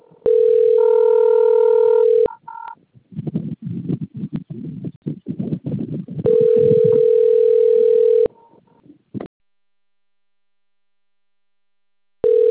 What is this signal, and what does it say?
Ringback tone (North America), 440 + 480 Hz.